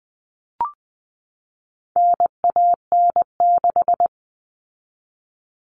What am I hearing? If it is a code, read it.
Morse code, heard as a keyed tone, about 20 wpm, 700 Hz. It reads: NAN6
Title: All Cut Numbers - 4-Digit 20wpm